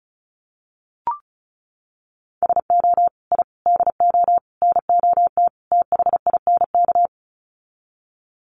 Morse code, heard as a keyed tone, about 34 wpm, 700 Hz. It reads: SO I DO NOT THINK
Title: Sets of 5 Words 35wpm